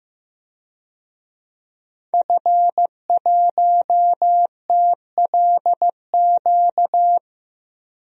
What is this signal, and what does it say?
Morse code, heard as a keyed tone, about 16 wpm, 700 Hz. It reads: F1TLQ
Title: DXCC-100 Call Signs - 5 Characters - 15wpm